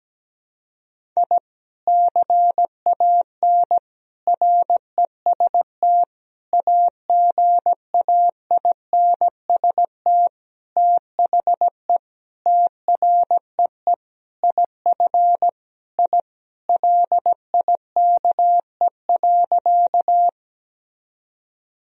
Morse code, heard as a keyed tone, about 17 wpm, 700 Hz. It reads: I CAN REST AGAINST THE TREE IF I LIKE.
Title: Sentences from Top 1000 Words - Review Part 1 - 17wpm